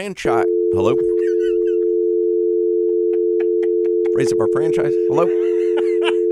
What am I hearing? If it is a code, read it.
Dial tone (North America), 350 + 440 Hz, with speech.